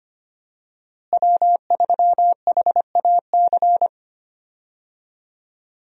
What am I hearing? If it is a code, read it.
Morse code, heard as a keyed tone, about 25 wpm, 700 Hz. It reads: W3HAC
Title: USA Call Signs - 5 Characters - Repeated 3x - 25wpm